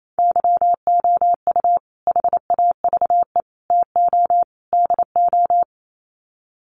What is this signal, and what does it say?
Morse code, heard as a keyed tone, about 28 wpm, 700 Hz. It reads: YOU HAVE TO DO
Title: Sets of 4 Words 28wpm